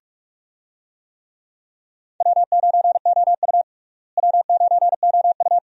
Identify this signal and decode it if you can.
Morse code, heard as a keyed tone, about 42 wpm, 700 Hz. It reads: W9OU W9OU
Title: USA Call Signs - 4 Characters - Repeated 2x - 45wpm